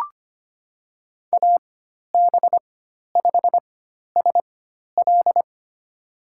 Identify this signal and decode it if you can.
Morse code, heard as a keyed tone, about 25 wpm, spaced as for 12 wpm, 700 Hz. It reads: AB5SL
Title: USA Call Signs - 5 Characters - 4x Character Spacing - 25wpm